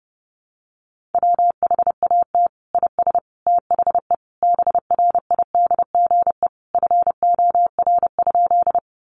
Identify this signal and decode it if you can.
Morse code, heard as a keyed tone, about 30 wpm, 700 Hz. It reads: WHAT IS THE BRIDGE FOR?
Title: Sentences from Top 1200 Words 30wpm